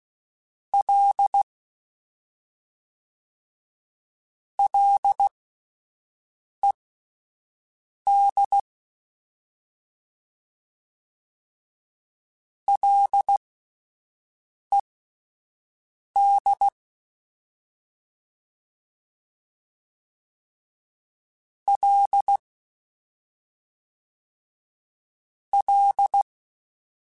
Morse code, heard as a keyed tone, about 16 wpm, spaced as for 5 wpm, 800 Hz. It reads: L LED LED L L
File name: Ls / Single L m3s / Single L Prefixe 3x2 Twice @5